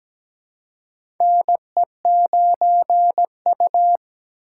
Morse code, heard as a keyed tone, about 17 wpm, 700 Hz. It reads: NE9U
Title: K1USN SST QSO Practice - 17wpm